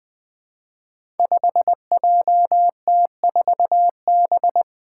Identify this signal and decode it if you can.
Morse code, heard as a keyed tone, about 20 wpm, 700 Hz. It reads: 5JT4B
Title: Five Challenging Letter-Number Combinations 20wpm